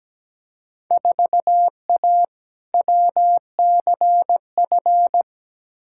Morse code, heard as a keyed tone, about 17 wpm, 700 Hz. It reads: 4A WCF
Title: Field Day Exchanges - 17wpm